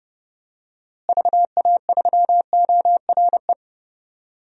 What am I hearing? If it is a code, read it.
Morse code, heard as a keyed tone, about 32 wpm, 700 Hz. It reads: VA3ORE
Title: DXCC-100 Call Signs - 6 Characters - Repeated 2x - 30wpm